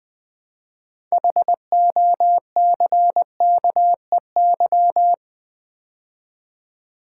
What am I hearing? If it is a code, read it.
Morse code, heard as a keyed tone, about 20 wpm, 700 Hz. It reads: HOCKEY
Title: Build A Word - 3 Letters At A Time - 6-Letter Words - 20wpm